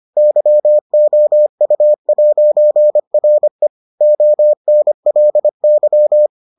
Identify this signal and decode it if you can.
Morse code, heard as a keyed tone, about 25 wpm, 600 Hz. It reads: YOU'RE ONLY